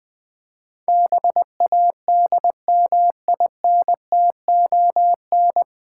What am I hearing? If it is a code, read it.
Morse code, heard as a keyed tone, about 20 wpm, 700 Hz. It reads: BADMINTON
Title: Difficult Words - Speed-Racing with Hints - 20wpm